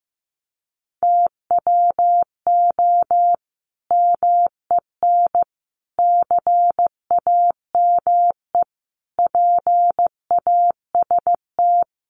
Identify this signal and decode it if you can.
Morse code, heard as a keyed tone, about 15 wpm, 700 Hz. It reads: TWO MEN CAME PAST
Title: Sentences from Top 600 Words 15wpm